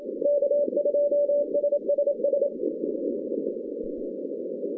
Morse code, heard as a keyed tone, about 27 wpm, 580 Hz. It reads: K2SSS
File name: 2015 cq ww cw / 2015 cqwww k5tr - 0-18